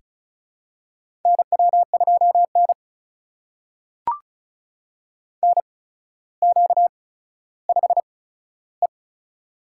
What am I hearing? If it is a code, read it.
Morse code, heard as a keyed tone, about 36 wpm, spaced as for 15 wpm, 700 Hz. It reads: NW2N NQ5E
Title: USA Call Signs - 4 Characters - 8x Character Spacing - 35wpm